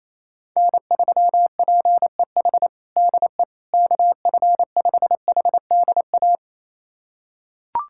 Morse code, heard as a keyed tone, about 28 wpm, 700 Hz. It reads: N3PEH DE KF5HDA
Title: USA Callsign DE Callsign 28wpm